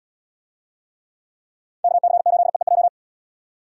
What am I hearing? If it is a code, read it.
Morse code, heard as a keyed tone, about 72 wpm, 700 Hz. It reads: KC8EEJ